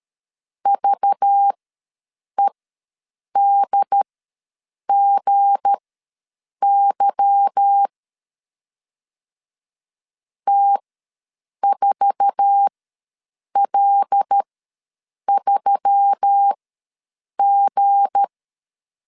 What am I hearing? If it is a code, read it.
Morse code, heard as a keyed tone, about 13 wpm, spaced as for 7 wpm, 800 Hz. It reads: VEDGY T4L3G
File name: LEZIONI / 8 YJ45 / 8 YJ4 ws 12wpm 3249